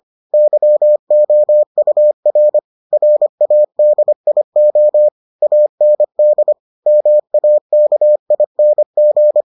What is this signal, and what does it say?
Morse code, heard as a keyed tone, about 25 wpm, 600 Hz. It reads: YOUR RADIO AND MAKING